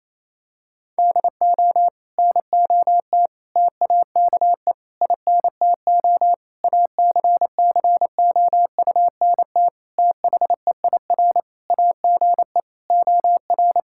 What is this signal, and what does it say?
Morse code, heard as a keyed tone, about 28 wpm, 700 Hz. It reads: DO NOT TAKE INTO ACCOUNT THEIR AGE OR
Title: Sentences from Top 1600 Words 28wpm